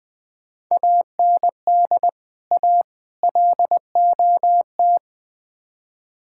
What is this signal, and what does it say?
Morse code, heard as a keyed tone, about 20 wpm, 700 Hz. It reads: AND A LOT
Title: Build A Phrase - 5-Word Phrases - 20wpm